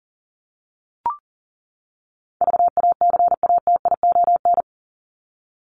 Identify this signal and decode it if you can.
Morse code, heard as a keyed tone, about 40 wpm, 700 Hz. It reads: VACATION
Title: Top 1501-1750 Words 40wpm